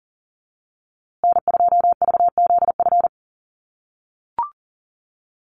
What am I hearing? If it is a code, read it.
Morse code, heard as a keyed tone, about 40 wpm, 700 Hz. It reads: N2VZF
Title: USA Call Signs Repeated 3x 40wpm